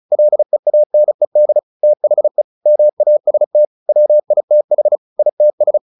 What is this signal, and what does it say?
Morse code, heard as a keyed tone, about 34 wpm, 600 Hz. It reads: LEANED THE MAST WITH ITS
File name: the old man and the sea / the old man and the sea 1 35wpm